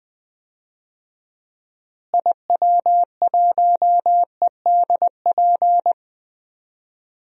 Morse code, heard as a keyed tone, about 20 wpm, 700 Hz. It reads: IW1EDP
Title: DXCC-100 Call Signs - 6 Characters - Repeated 3x - 20wpm